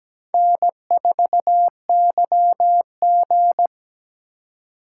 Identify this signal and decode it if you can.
Morse code, heard as a keyed tone, about 17 wpm, 700 Hz. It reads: N4YG
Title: USA Call Signs - 4 Characters - 2x Character Spacing - 17wpm